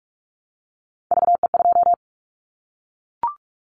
Morse code, heard as a keyed tone, about 44 wpm, 700 Hz. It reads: VE2
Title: Callsign Prefixes for Top 100 DXCC - 45wpm